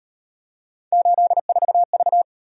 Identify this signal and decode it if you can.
Morse code, heard as a keyed tone, about 37 wpm, 700 Hz. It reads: 84V